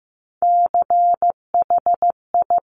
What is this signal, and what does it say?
Morse code, heard as a keyed tone, about 15 wpm, 690 Hz. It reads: CHI
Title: Top Three Letter Combinations in Words 15wpm